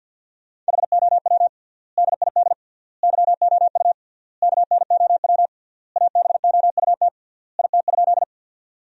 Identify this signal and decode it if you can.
Morse code, heard as a keyed tone, about 50 wpm, 700 Hz. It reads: HOW DID YOU KNOW ABOUT IT?
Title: 3x Word-Spacing - Sentences from Top 100 Words - 50wpm